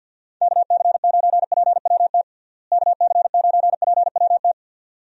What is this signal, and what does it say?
Morse code, heard as a keyed tone, about 51 wpm, 700 Hz. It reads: KC9PWT KC9PWT
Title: USA Call Signs - 6 Characters - Repeated 2x - 50wpm